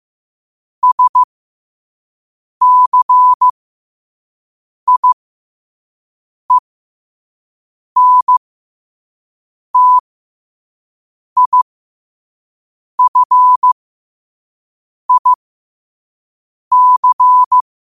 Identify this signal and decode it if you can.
Morse code, heard as a keyed tone, about 15 wpm, spaced as for 5 wpm, 1000 Hz. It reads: SCIENTIFIC